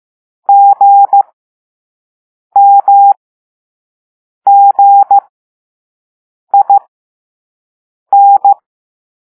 Morse code, heard as a keyed tone, about 15 wpm, spaced as for 5 wpm, 800 Hz. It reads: GMGIN